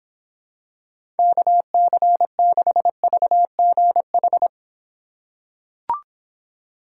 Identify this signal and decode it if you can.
Morse code, heard as a keyed tone, about 25 wpm, 700 Hz. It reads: KC6VGH